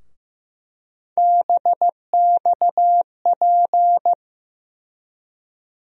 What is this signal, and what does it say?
Morse code, heard as a keyed tone, about 15 wpm, 700 Hz. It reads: BXP
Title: USA Call Sign Postfixes - 3 Letters 15wpm